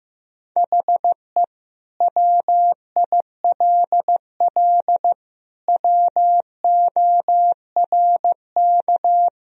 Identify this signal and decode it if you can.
Morse code, heard as a keyed tone, about 15 wpm, 700 Hz. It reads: HE WILL WORK
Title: Sentences from Top 1000 Words - Review Part 1 - 15wpm